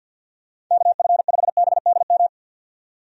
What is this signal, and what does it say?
Morse code, heard as a keyed tone, about 51 wpm, 700 Hz. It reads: KF5BDM
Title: USA Call Signs - 6 Characters - 50wpm